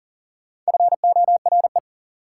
Morse code, heard as a keyed tone, about 40 wpm, 700 Hz. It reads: FORE
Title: Prefixes and Suffixes 40wpm